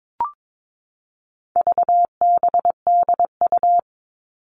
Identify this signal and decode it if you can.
Morse code, heard as a keyed tone, about 22 wpm, 700 Hz. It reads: VBDU
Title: All Cut Numbers - 4-Digit 22wpm